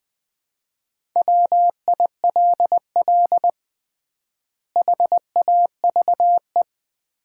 Morse code, heard as a keyed tone, about 20 wpm, 700 Hz. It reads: WILL HAVE
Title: 3x Word-Spacing - Sets of 3 Words - 20wpm